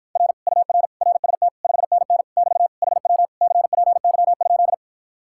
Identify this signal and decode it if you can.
Morse code, heard as a keyed tone, about 53 wpm, 700 Hz. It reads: R UR RST 5NN <BT> HW CPY?